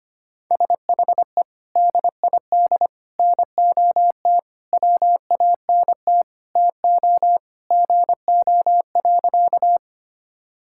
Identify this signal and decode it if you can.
Morse code, heard as a keyed tone, about 25 wpm, 700 Hz. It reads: SHE DID NOT WANT TO GO.
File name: practice / 25wpm / Sentences with 4-Word Phrases - 25wpm